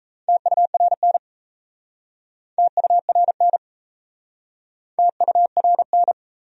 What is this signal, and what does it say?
Morse code, heard as a keyed tone, about 37 wpm, 700 Hz. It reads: TURN TURN TURN